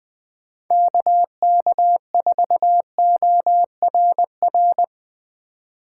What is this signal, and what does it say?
Morse code, heard as a keyed tone, about 20 wpm, 700 Hz. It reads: KK4ORR
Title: USA Call Signs - 6 Characters - 8x Character Spacing - 20wpm